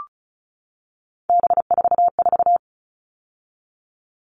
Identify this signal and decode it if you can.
Morse code, heard as a keyed tone, about 34 wpm, 700 Hz. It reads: B44
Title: All Cut Numbers - 3-Digit 35wpm